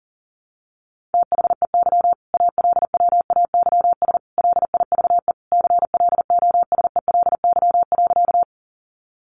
Morse code, heard as a keyed tone, about 40 wpm, 700 Hz. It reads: THEY ALWAYS LIVE CLOSELY.
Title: Sentences from Top 1400 Words 40wpm